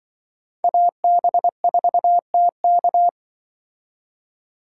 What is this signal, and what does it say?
Morse code, heard as a keyed tone, about 23 wpm, 700 Hz. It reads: AB4TK
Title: Speed Racing - USA 2x2 Call Signs 20wpm